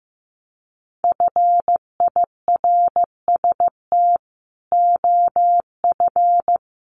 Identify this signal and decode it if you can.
Morse code, heard as a keyed tone, about 15 wpm, 700 Hz. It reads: FIRST OF
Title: Sentences from Top 1500 Words 15wpm